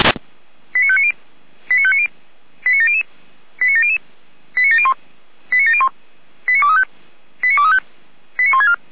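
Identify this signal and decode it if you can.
Five-tone selective call (ZVEI), heard as quick runs of five tones, tones of about 70 ms. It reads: 89590 89590 89700 89700 89701 89701 89235 89235 89164
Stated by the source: ZVEI five-tone selective call received by SDR.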